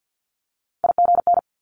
Morse code, heard as a keyed tone, about 50 wpm, 700 Hz. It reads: IGN
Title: Multi-letter Phonemes Quick Fire 50wpm